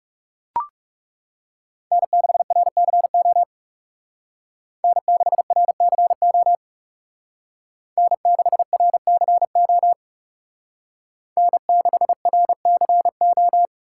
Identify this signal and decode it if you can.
Morse code, heard as a keyed tone, about 37 wpm, 700 Hz. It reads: N6RCO N6RCO N6RCO N6RCO
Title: Speed Racing - USA Call Signs 30wpm